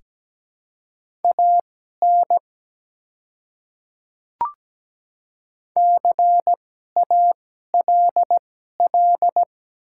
Morse code, heard as a keyed tone, about 17 wpm, spaced as for 12 wpm, 700 Hz. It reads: AN CALL
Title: 2x Character Spacing - Top 100 Words - 17wpm